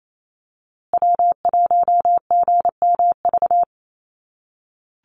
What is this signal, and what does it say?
Morse code, heard as a keyed tone, about 28 wpm, 700 Hz. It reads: W1GMV
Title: USA Call Signs Repeated 3x 28wpm